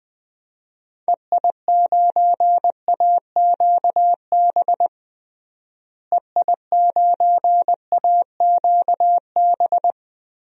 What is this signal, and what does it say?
Morse code, heard as a keyed tone, about 20 wpm, 700 Hz. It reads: EI9AQB EI9AQB
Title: DXCC-100 Call Signs - 6 Characters - Repeated 2x - 20wpm